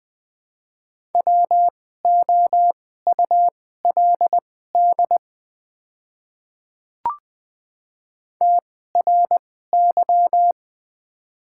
Morse code, heard as a keyed tone, about 20 wpm, spaced as for 15 wpm, 690 Hz. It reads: WOULD TRY
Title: 2x Character Spacing - Top 500 Words - 20wpm